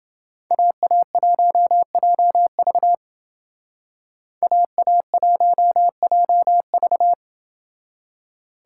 Morse code, heard as a keyed tone, about 29 wpm, 700 Hz. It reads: AA1JV AA1JV